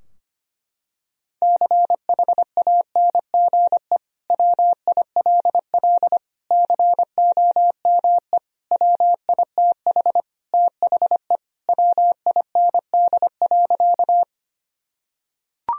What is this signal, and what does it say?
Morse code, heard as a keyed tone, about 25 wpm, 700 Hz. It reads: CHANGE WILL COME WITH THE WIND.
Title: Sentences from Top 1000 Words - Review Part 2 - 25wpm